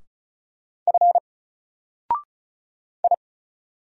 Morse code, heard as a keyed tone, about 35 wpm, 700 Hz. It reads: F I
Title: Single Character (Letter:Number:Punctuation:Prosign) 35wpm